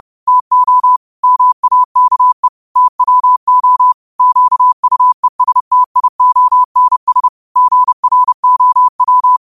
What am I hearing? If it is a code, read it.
Morse code, heard as a keyed tone, about 30 wpm, 1000 Hz. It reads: TO MAKE TWO QUESTIONS GROW
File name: qotdfiles / 30wpm